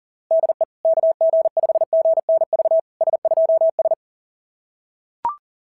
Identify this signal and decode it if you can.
Morse code, heard as a keyed tone, about 38 wpm, 690 Hz. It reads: DE KG5GNV S2S
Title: SOTA QSO Practice 40wpm